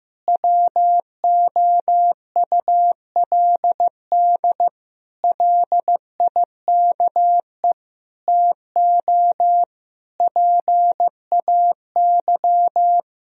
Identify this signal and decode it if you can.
Morse code, heard as a keyed tone, about 16 wpm, 700 Hz. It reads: WOULD LIKE TO PAY
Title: Sentences from Top 1000 Words - Review Part 2 - 15wpm